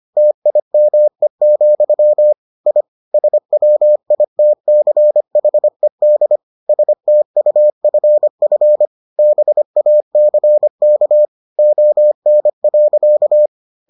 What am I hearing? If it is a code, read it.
Morse code, heard as a keyed tone, about 25 wpm, 600 Hz. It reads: TIME, I SWITCHED STUFF BACK ON.